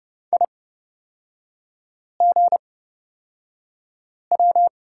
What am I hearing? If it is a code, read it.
Morse code, heard as a keyed tone, about 30 wpm, spaced as for 5 wpm, 700 Hz. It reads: IGW
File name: Lesson 36e - BT - Three Characters 30wpm